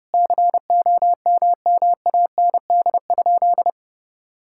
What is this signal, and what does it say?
Morse code, heard as a keyed tone, about 28 wpm, 700 Hz. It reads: COMMAND?
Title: Sentences from Top 1600 Words 30wpm